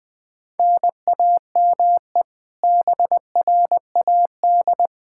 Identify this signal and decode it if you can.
Morse code, heard as a keyed tone, about 21 wpm, 700 Hz. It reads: NAME BRAD
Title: Practice QSO - Exchange 1 - Beginning - 20wpm